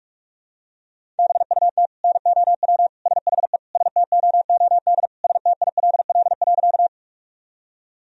Morse code, heard as a keyed tone, about 41 wpm, 700 Hz. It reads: BUT NOW SHE STOOD STILL.
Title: Sentences from Top 1000 Words - Review Part 2 - 45wpm